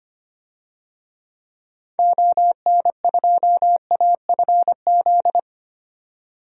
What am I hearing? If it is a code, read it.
Morse code, heard as a keyed tone, about 25 wpm, 700 Hz. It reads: ON2AFZ
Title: DXCC-100 Call Signs - 6 Characters - 25wpm